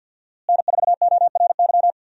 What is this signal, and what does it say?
Morse code, heard as a keyed tone, about 47 wpm, 700 Hz. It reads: N4ORY
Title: USA Call Signs - 5 Characters - Repeated 2x - 50wpm